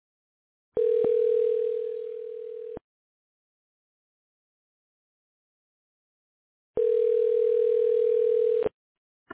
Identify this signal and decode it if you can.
Ringback tone (North America), 440 + 480 Hz.